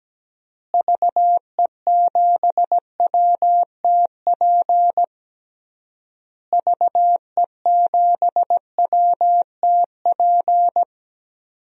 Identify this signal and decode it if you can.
Morse code, heard as a keyed tone, about 17 wpm, 700 Hz. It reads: VE7WTP VE7WTP
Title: DXCC-100 Call Signs - 6 Characters - Repeated 3x - 17wpm